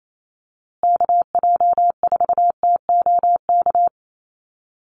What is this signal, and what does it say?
Morse code, heard as a keyed tone, about 28 wpm, 690 Hz. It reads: KJ4TOK
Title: USA Call Signs Repeated 3x 28wpm